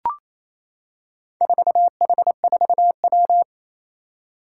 Morse code, heard as a keyed tone, about 28 wpm, 700 Hz. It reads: 4H4W
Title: Four Challenging Letter-Number Combinations 28wpm